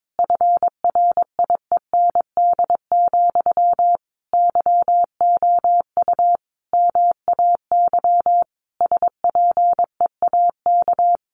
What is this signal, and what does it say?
Morse code, heard as a keyed tone, about 22 wpm, 700 Hz. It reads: FRIEND, YOU MAY SPEAK
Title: Sentences from Top 1300 Words 22wpm